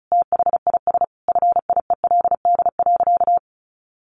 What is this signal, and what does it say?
Morse code, heard as a keyed tone, about 34 wpm, 700 Hz. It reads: THIS FIELD.
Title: Sentences from Top 1200 Words 35wpm